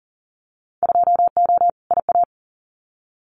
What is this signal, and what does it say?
Morse code, heard as a keyed tone, about 40 wpm, 700 Hz. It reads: 2O IA
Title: Winter Field Day Exchanges 40wpm